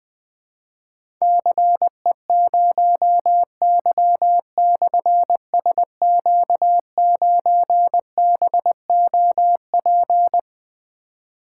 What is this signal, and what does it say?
Morse code, heard as a keyed tone, about 20 wpm, 700 Hz. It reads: CE0Y/SQ9BOP
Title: Difficult Call Signs 20wpm